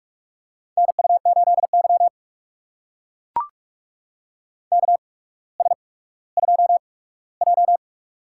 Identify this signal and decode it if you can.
Morse code, heard as a keyed tone, about 44 wpm, spaced as for 19 wpm, 700 Hz. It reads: NU8Y KS2J
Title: USA Call Signs - 4 Characters - 8x Character Spacing - 45wpm